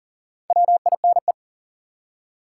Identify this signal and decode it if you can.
Morse code, heard as a keyed tone, about 40 wpm, 700 Hz. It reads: WINE
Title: Four Letter Words - 40wpm